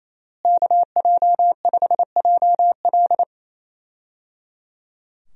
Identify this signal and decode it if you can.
Morse code, heard as a keyed tone, about 27 wpm, 700 Hz. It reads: KJ5JL
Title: USA Call Signs - 5 Characters - Repeated 3x - 28wpm